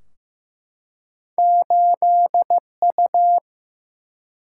Morse code, heard as a keyed tone, about 15 wpm, 700 Hz. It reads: 8U
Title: Single-Letter Single-Number 15wpm